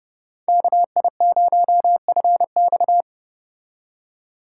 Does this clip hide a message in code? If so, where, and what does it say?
Yes: Morse code from 0.5 to 3.0 s, heard as a keyed tone, about 29 wpm, 700 Hz. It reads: KI0FX